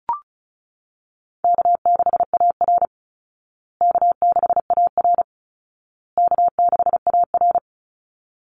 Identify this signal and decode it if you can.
Morse code, heard as a keyed tone, about 34 wpm, 700 Hz. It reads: K6AR K6AR K6AR
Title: Call Signs Repeated 2x - 35wpm